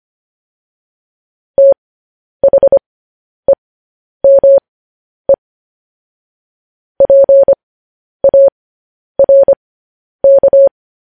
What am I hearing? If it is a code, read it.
Morse code, heard as a keyed tone, about 25 wpm, spaced as for 10 wpm, 550 Hz. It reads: THEME PARK
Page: News Headlines in Morse Code at 10 WPM